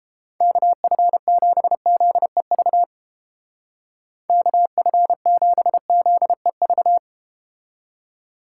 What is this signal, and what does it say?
Morse code, heard as a keyed tone, about 32 wpm, 700 Hz. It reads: KF7ZEV KF7ZEV